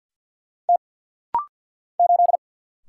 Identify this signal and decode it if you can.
Morse code, heard as a keyed tone, about 49 wpm, 700 Hz. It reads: T 8
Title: Single Letter-Number Alt - Rapid-Fire 50wpm